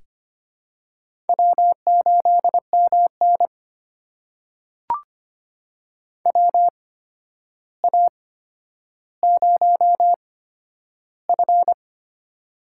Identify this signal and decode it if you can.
Morse code, heard as a keyed tone, about 25 wpm, spaced as for 11 wpm, 700 Hz. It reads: W8MN WA0F
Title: USA Call Signs - 4 Characters - 8x Character Spacing - 25wpm